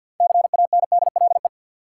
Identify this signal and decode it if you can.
Morse code, heard as a keyed tone, about 53 wpm, 700 Hz. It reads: CANDLE
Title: Six Letter Words - 50wpm